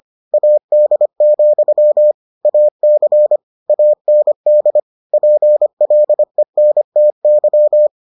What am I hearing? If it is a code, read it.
Morse code, heard as a keyed tone, about 25 wpm, 600 Hz. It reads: AD, AC AND PLENTY